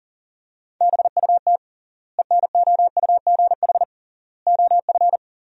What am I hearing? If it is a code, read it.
Morse code, heard as a keyed tone, about 40 wpm, 700 Hz. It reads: BUT ENOUGH OF
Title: 3x Word-Spacing - Sentences from Top 400 Words - 40wpm